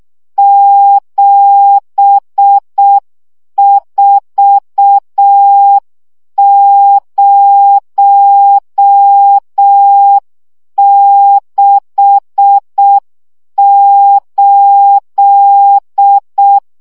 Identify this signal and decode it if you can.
Morse code, heard as a keyed tone, about 6 wpm, 800 Hz. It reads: 74068
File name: morze vizsga 2011-3 5